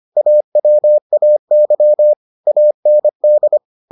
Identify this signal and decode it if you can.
Morse code, heard as a keyed tone, about 25 wpm, 600 Hz. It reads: AWAY AND